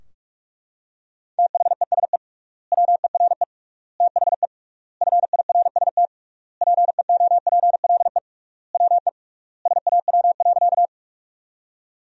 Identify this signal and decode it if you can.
Morse code, heard as a keyed tone, about 43 wpm, 700 Hz. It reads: THESE WERE THE FIRST PEOPLE WE SAW.
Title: 3x Word-Spacing - Sentences from Top 300 Words - 45wpm